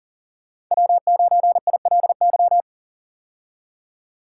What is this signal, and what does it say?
Morse code, heard as a keyed tone, about 37 wpm, 700 Hz. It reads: W9ILY
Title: K1USN SST QSO Practice - 40wpm